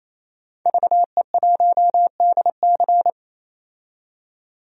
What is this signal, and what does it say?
Morse code, heard as a keyed tone, about 27 wpm, 700 Hz. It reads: VE1DC
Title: DXCC-100 Call Signs - 5 Characters - 28wpm